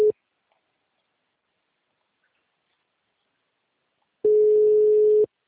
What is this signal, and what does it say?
Ringback tone (Europe, 425 Hz).